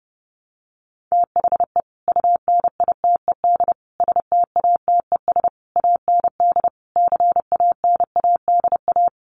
Morse code, heard as a keyed tone, about 30 wpm, 700 Hz. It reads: THE UNITED STATES AND CANADA
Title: Sets of 5 Words 30wpm